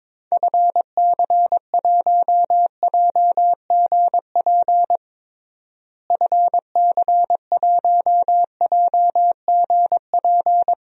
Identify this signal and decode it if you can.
Morse code, heard as a keyed tone, about 21 wpm, 700 Hz. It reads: FC1JGP FC1JGP
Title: DXCC-100 Call Signs - 6 Characters - Repeated 3x - 22wpm